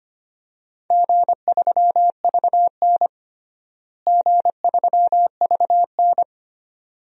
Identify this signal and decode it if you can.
Morse code, heard as a keyed tone, about 25 wpm, 700 Hz. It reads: G3VN G3VN
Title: DXCC-100 Call Signs - 4 Characters - Repeated 2x - 25wpm